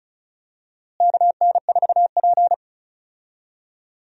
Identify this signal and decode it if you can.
Morse code, heard as a keyed tone, about 36 wpm, 700 Hz. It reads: KN4P
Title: USA Call Signs - 4 Characters - Repeated 3x - 35wpm